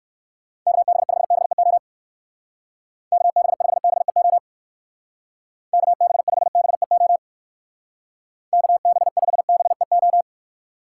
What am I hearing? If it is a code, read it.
Morse code, heard as a keyed tone, about 59 wpm, 700 Hz. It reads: KB5BEO KB5BEO KB5BEO KB5BEO